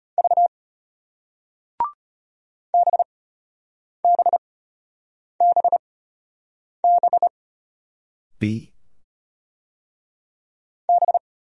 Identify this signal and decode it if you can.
Morse code, heard as a keyed tone, about 35 wpm, spaced as for 8 wpm, 700 Hz. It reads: V BBBB B